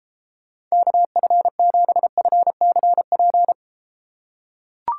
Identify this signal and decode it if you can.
Morse code, heard as a keyed tone, about 33 wpm, 700 Hz. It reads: KF7FCP